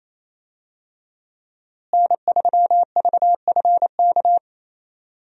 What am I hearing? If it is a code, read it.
Morse code, heard as a keyed tone, about 27 wpm, 700 Hz. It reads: N3VFK
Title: USA Call Signs - 5 Characters - Repeated 2x - 28wpm